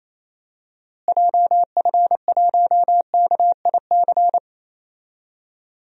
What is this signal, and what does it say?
Morse code, heard as a keyed tone, about 27 wpm, 700 Hz. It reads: JF1KIC
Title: DXCC-100 Call Signs - 6 Characters - Repeated 3x - 28wpm